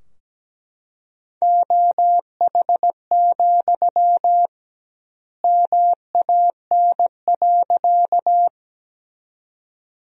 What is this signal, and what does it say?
Morse code, heard as a keyed tone, about 17 wpm, 700 Hz. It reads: OH, MAN.
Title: 2x Word-Spacing - Sentences from Top 500 Words - 17wpm